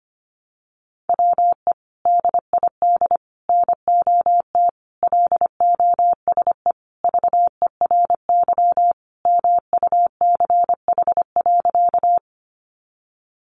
Morse code, heard as a keyed tone, about 25 wpm, 700 Hz. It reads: WE DID NOT LOSE VERY MUCH.
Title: Sentences from Top 1100 Words 25wpm